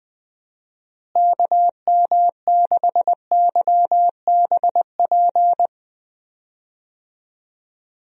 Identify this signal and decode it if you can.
Morse code, heard as a keyed tone, about 21 wpm, 700 Hz. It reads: KM6YBP
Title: USA Call Signs - 6 Characters - Repeated 3x - 20wpm